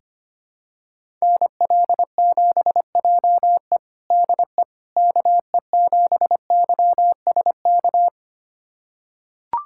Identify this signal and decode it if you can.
Morse code, heard as a keyed tone, about 25 wpm, 700 Hz. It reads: NL7JE DE KE7YSK
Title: USA Callsign DE Callsign 25wpm